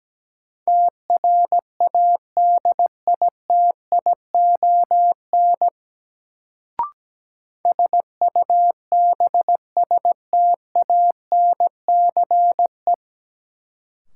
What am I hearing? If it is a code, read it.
Morse code, heard as a keyed tone, about 17 wpm, 700 Hz. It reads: TRADITION SUBSTANCE
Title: Sentences from Top 1500 Words - New Word Drill Alt - 17wpm